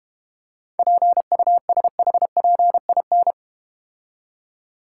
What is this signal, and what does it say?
Morse code, heard as a keyed tone, about 31 wpm, 700 Hz. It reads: PUSHPIN
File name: Easy Words - Speed-Racing with Hints - 25wpm